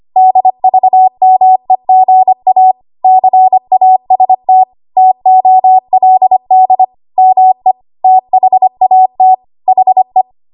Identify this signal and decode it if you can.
Morse code, heard as a keyed tone, about 25 wpm, 750 Hz. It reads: DVMEGA CAST TOLD ME THAT HE
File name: Archive / 25 WPM / 25WPM